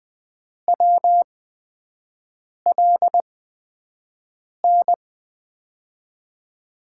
Morse code, heard as a keyed tone, about 20 wpm, spaced as for 5 wpm, 700 Hz. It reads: WLN
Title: USA Call Signs - 6 Characters - 8x Character Spacing - 20wpm